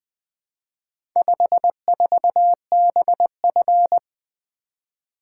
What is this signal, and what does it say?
Morse code, heard as a keyed tone, about 22 wpm, 700 Hz. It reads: 54BF
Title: Speed-Racing - Four Letter-Number Combinations - 20wpm